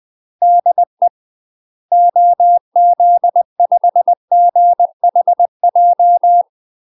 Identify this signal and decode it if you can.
Morse code, heard as a keyed tone, about 20 wpm, 700 Hz. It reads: DE OZ5GHJ
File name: UK MASTER QSOs TRAINING / 20WPM / UK MASTER QSOs 0008